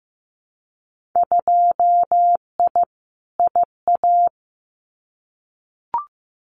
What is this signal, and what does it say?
Morse code, heard as a keyed tone, about 15 wpm, 700 Hz. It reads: 2I IA
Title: Winter Field Day Exchanges Encore 15wpm